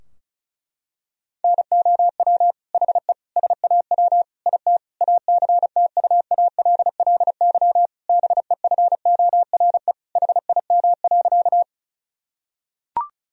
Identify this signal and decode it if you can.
Morse code, heard as a keyed tone, about 36 wpm, 700 Hz. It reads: NOW HE SAW IT ACTUALLY BEFORE HIM.
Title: Sentences from Top 1000 Words - Review Part 4 - 35wpm